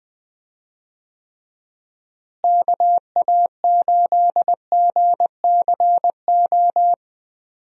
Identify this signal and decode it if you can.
Morse code, heard as a keyed tone, about 20 wpm, 700 Hz. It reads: KA8GCO
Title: USA Call Signs - 6 Characters - Repeated 2x - 20wpm